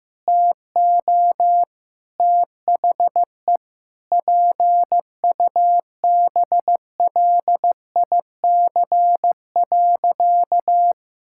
Morse code, heard as a keyed tone, about 15 wpm, 700 Hz. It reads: TO THE PUBLIC.
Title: Sentences from Top 1000 Words - Review Part 6 - 15wpm